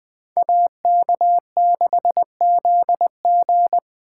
Morse code, heard as a keyed tone, about 19 wpm, 700 Hz. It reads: AK6ZG